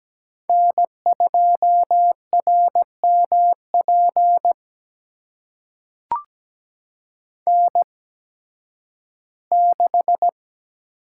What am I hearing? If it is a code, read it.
Morse code, heard as a keyed tone, about 17 wpm, spaced as for 11 wpm, 700 Hz. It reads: N2RMP N6
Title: USA Call Signs - 5 Characters - 8x Character Spacing - 17wpm